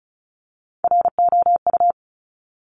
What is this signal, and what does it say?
Morse code, heard as a keyed tone, about 36 wpm, 700 Hz. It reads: ROU
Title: Top Three Letter Combinations in Words 35wpm